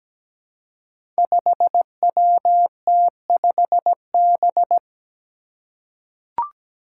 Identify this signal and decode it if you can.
Morse code, heard as a keyed tone, about 17 wpm, 700 Hz. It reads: 5WT5B
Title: Five Challenging Letter-Number Combinations 17wpm